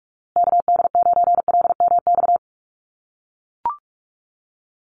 Morse code, heard as a keyed tone, about 44 wpm, 690 Hz. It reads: KD9LMX
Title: USA Call Signs Repeated 3x 45wpm